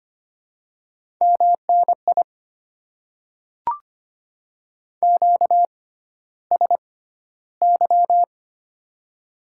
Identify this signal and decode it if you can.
Morse code, heard as a keyed tone, about 25 wpm, spaced as for 13 wpm, 700 Hz. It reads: MNI QSY
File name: Top 90 QSO Elements - 6x Character Spacing - 25wpm